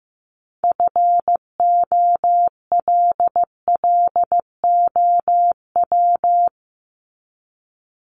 Morse code, heard as a keyed tone, about 15 wpm, 700 Hz. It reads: FOLLOW
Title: ING Suffix - Top 100 Words with Base Words 15wpm